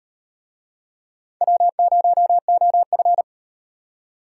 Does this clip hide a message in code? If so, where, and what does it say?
Yes: Morse code from 1.4 to 3.2 s, heard as a keyed tone, about 35 wpm, 700 Hz. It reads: W0OF